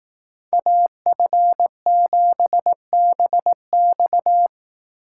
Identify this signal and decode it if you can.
Morse code, heard as a keyed tone, about 18 wpm, 700 Hz. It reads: AF7BX
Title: Speed Racing - USA 2x2 Call Signs 15wpm